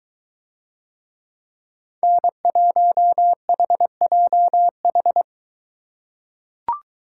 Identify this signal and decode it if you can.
Morse code, heard as a keyed tone, about 22 wpm, 700 Hz. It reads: N1HJH